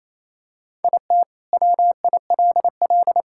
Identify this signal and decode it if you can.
Morse code, heard as a keyed tone, about 28 wpm, 700 Hz. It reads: IT WILL
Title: Sentences from Top 1600 Words 28wpm